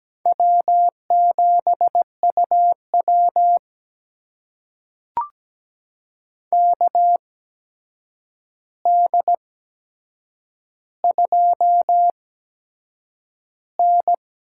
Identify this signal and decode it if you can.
Morse code, heard as a keyed tone, about 17 wpm, spaced as for 7 wpm, 700 Hz. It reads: W7UW KD2N
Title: USA Call Signs - 4 Characters - 8x Character Spacing - 17wpm